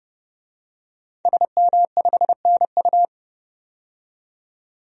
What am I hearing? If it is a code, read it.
Morse code, heard as a keyed tone, about 30 wpm, 700 Hz. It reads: SM5NU
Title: DXCC-100 Call Signs - 5 Characters - 30wpm